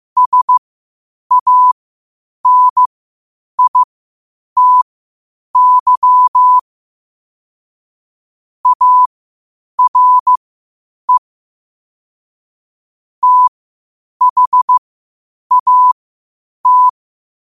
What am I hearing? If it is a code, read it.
Morse code, heard as a keyed tone, about 15 wpm, spaced as for 8 wpm, 1000 Hz. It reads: SANITY ARE THAT
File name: qotdfiles / 8wpm